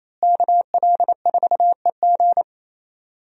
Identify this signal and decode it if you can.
Morse code, heard as a keyed tone, about 27 wpm, 700 Hz. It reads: KL4EG
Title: USA Call Signs - 5 Characters - Repeated 2x - 28wpm